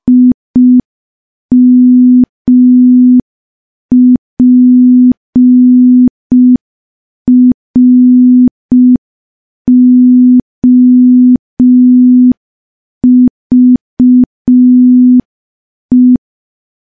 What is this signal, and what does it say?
Morse code, heard as a keyed tone, about 5 wpm, 250 Hz. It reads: IMPROVE
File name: morse / morse-training-05WPM